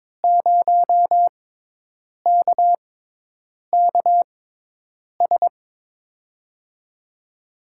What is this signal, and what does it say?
Morse code, heard as a keyed tone, about 22 wpm, spaced as for 8 wpm, 700 Hz. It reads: 0KKS
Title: USA Call Signs - 6 Characters - 6x Character Spacing - 22wpm